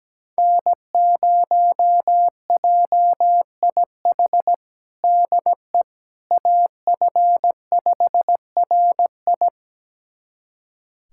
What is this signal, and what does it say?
Morse code, heard as a keyed tone, about 17 wpm, 700 Hz. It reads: N0JIH DE AF5RI
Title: USA Callsign DE Callsign - 5 Characters - 17wpm